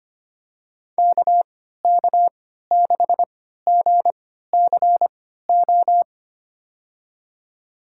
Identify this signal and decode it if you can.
Morse code, heard as a keyed tone, about 25 wpm, spaced as for 14 wpm, 700 Hz. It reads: KK6GCO
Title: USA Call Signs - 6 Characters - 3x Character Spacing - 25wpm